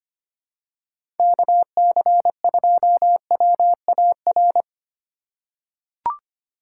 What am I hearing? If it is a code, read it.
Morse code, heard as a keyed tone, about 25 wpm, 700 Hz. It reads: KC2WAR
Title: USA Call Signs - 2x Character Spacing - 25wpm